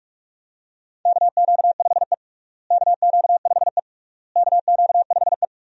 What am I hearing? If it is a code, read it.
Morse code, heard as a keyed tone, about 44 wpm, 700 Hz. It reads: KQ5E KQ5E KQ5E
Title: USA Call Signs - 4 Characters - Repeated 3x - 45wpm